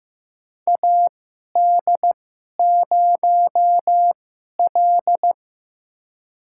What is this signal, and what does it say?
Morse code, heard as a keyed tone, about 15 wpm, spaced as for 11 wpm, 700 Hz. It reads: AD0L
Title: USA Call Signs - 4 Characters - 2x Character Spacing - 15wpm